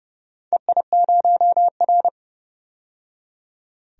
Morse code, heard as a keyed tone, about 31 wpm, 700 Hz. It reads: EI0R
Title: DXCC-100 Call Signs - 4 Characters - 30wpm